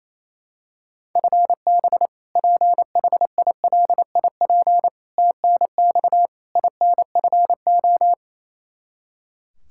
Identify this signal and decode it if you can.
Morse code, heard as a keyed tone, about 28 wpm, 700 Hz. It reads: FB PHILIP TNX INFO
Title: Practice QSO - Exchange 3 - More Info - 28wpm